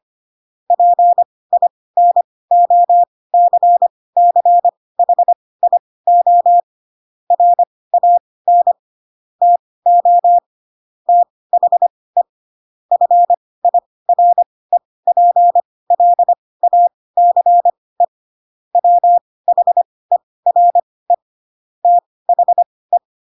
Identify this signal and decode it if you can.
Morse code, heard as a keyed tone, about 25 wpm, spaced as for 18 wpm, 700 Hz. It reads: PINOCCHIO RAN TO THE FIREPLACE WHERE THE
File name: Pinocchio / Pinocchio ch5 12 25 ik1hge